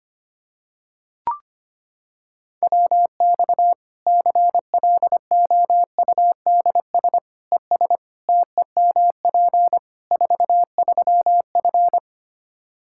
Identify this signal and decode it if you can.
Morse code, heard as a keyed tone, about 25 wpm, 700 Hz. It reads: WX CLOUDS ES TEMP 43F
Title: Practice QSO - Exchange 2 - Info - 25wpm